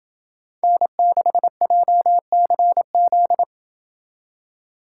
Morse code, heard as a keyed tone, about 27 wpm, 700 Hz. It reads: N6JCZ